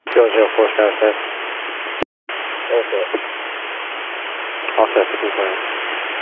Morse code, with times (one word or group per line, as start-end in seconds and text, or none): none